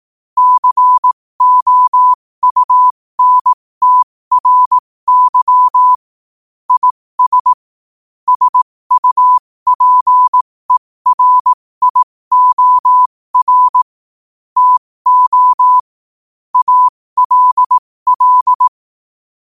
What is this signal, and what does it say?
Morse code, heard as a keyed tone, about 18 wpm, spaced as for 15 wpm, 1000 Hz. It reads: COUNTRY IS SUPERIOR TO ALL
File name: qotdfiles / 15wpm